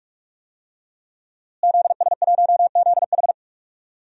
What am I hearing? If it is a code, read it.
Morse code, heard as a keyed tone, about 46 wpm, 700 Hz. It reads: ZS1ZH
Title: DXCC-100 Call Signs - 5 Characters - Repeated 2x - 45wpm